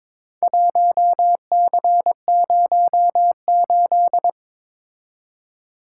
Morse code, heard as a keyed tone, about 21 wpm, 700 Hz. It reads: 1C08
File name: practice / 22wpm / Four Letter-Number Combinations - 22wpm